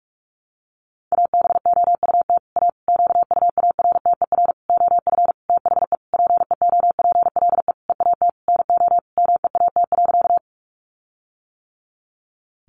Morse code, heard as a keyed tone, about 44 wpm, 700 Hz. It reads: ABOUT A QUARTER OF THE PEOPLE EAT NO MEAT.
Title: Sentences from Top 1300 Words 45wpm